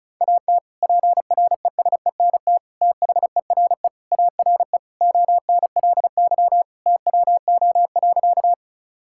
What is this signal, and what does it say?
Morse code, heard as a keyed tone, about 34 wpm, 700 Hz. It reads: AT PRESENT THERE ARE ONLY TWO.
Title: Sentences from Top 1000 Words - Review Part 5 - 35wpm